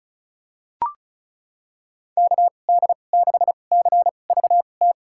Morse code, heard as a keyed tone, about 36 wpm, spaced as for 25 wpm, 700 Hz. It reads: KD6CVT
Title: USA Call Signs - 6 Characters - 2x Character Spacing - 35wpm